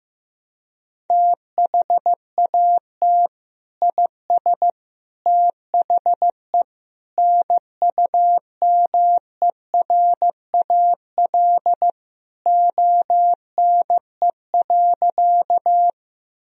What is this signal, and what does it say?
Morse code, heard as a keyed tone, about 15 wpm, 700 Hz. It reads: THAT IS THE NUMERAL ONE.
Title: Sentences from Top 1000 Words - Review Part 2 - 15wpm